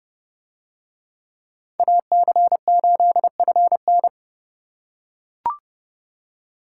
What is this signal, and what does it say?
Morse code, heard as a keyed tone, about 29 wpm, 700 Hz. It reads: AC8FN